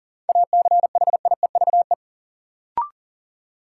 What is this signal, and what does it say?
Morse code, heard as a keyed tone, about 39 wpm, 700 Hz. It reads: ACHIEVE
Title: Sentences from Top 1000 Words - Review Part 5 - New Word Drill - 40wpm